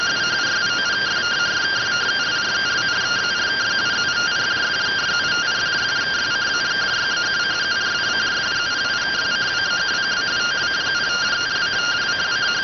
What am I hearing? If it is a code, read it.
Two-tone teleprinter signal, about 170 Hz shift.